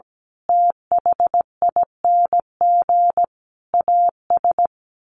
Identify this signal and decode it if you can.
Morse code, heard as a keyed tone, about 17 wpm, 700 Hz. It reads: THING AS
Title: Sets of 5 Words Encore 17wpm